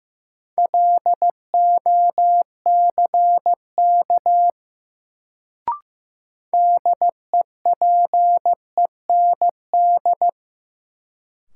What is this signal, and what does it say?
Morse code, heard as a keyed tone, about 15 wpm, 700 Hz. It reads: LOCK DEPEND
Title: Sentences from Top 1300 Words - New Word Drill - 15wpm